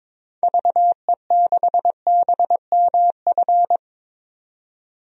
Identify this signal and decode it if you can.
Morse code, heard as a keyed tone, about 22 wpm, 700 Hz. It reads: VE6BMF
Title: DXCC-100 Call Signs - 6 Characters - 22wpm